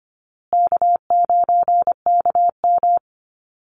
Morse code, heard as a keyed tone, about 25 wpm, 700 Hz. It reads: K9KM
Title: Call Signs Repeated 2x - 25wpm